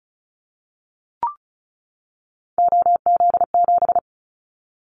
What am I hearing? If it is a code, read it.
Morse code, heard as a keyed tone, about 36 wpm, 700 Hz. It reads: OZ7
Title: Callsign Prefixes for Top 100 DXCC - 35wpm